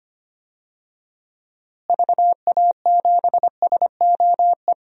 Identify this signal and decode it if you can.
Morse code, heard as a keyed tone, about 25 wpm, 700 Hz. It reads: VA7SOE
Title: DXCC-100 Call Signs - 6 Characters - Repeated 2x - 25wpm